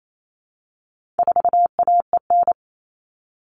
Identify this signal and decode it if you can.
Morse code, heard as a keyed tone, about 28 wpm, 700 Hz. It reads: 4AEN
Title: All Cut Numbers - 4-Digit 28wpm